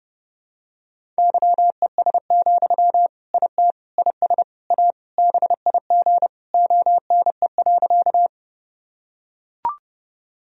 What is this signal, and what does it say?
Morse code, heard as a keyed tone, about 32 wpm, 700 Hz. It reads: YES, IT IS A BIG ONE.
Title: Sentences from Top 1000 Words - Review Part 1 - 30wpm